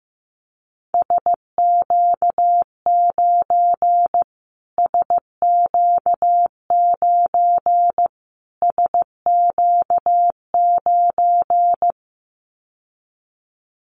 Morse code, heard as a keyed tone, about 15 wpm, 700 Hz. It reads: SQ9 SQ9 SQ9
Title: Callsign Prefixes for Top 25 DXCC - Repeated 3x - 15wpm